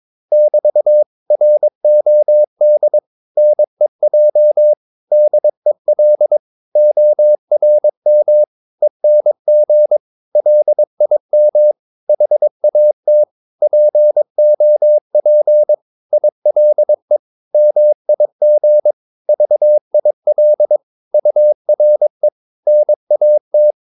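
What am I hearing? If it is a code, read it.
Morse code, heard as a keyed tone, about 22 wpm, 600 Hz. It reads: <BT> ROD NEJ DEL ORM ENG LIM HAT POP ILE MIG VIL URE NAT
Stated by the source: Morse practice of short Danish words at 22 wpm, from CW med Gnister.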